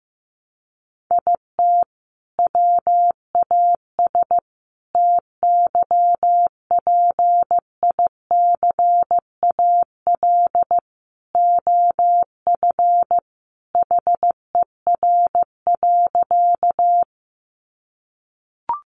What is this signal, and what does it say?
Morse code, heard as a keyed tone, about 15 wpm, 700 Hz. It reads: IT WAS TYPICAL OF HER.
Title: Sentences from Top 1100 Words 15wpm